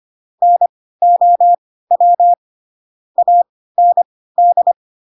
Morse code, heard as a keyed tone, about 25 wpm, spaced as for 16 wpm, 700 Hz. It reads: NOW AND
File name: Pinocchio / Pinocchio ch6 10 25 ik1hge